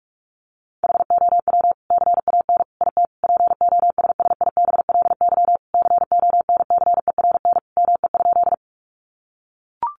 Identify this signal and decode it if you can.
Morse code, heard as a keyed tone, about 44 wpm, 700 Hz. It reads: HOW CAN IT POSSIBLY CONCERN ME?
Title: Sentences from Top 1300 Words 45wpm